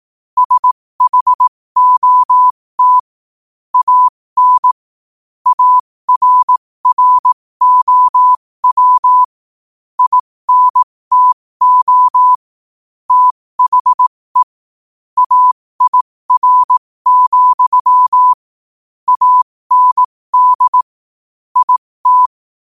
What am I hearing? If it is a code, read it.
Morse code, heard as a keyed tone, about 18 wpm, spaced as for 15 wpm, 1000 Hz. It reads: SHOT AN ARROW INTO THE AIR, AND IT
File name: qotdfiles / 15wpm